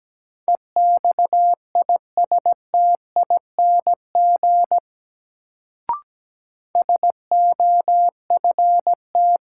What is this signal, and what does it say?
Morse code, heard as a keyed tone, about 17 wpm, 700 Hz. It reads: EXISTING SOFT
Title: Sentences from Top 1500 Words - New Word Drill - 17wpm